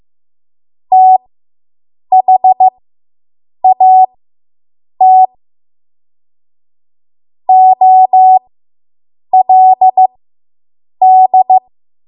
Morse code, heard as a keyed tone, about 15 wpm, spaced as for 7 wpm, 750 Hz. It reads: THAT OLD